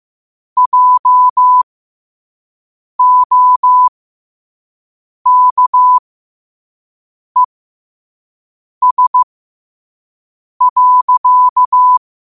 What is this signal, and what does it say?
Morse code, heard as a keyed tone, about 15 wpm, spaced as for 5 wpm, 1000 Hz. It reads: JOKES.
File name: qotdfiles / 5wpm